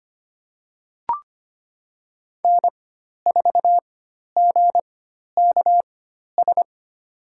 Morse code, heard as a keyed tone, about 25 wpm, spaced as for 12 wpm, 700 Hz. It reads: N4GKS
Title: USA Call Signs - 5 Characters - 4x Character Spacing - 25wpm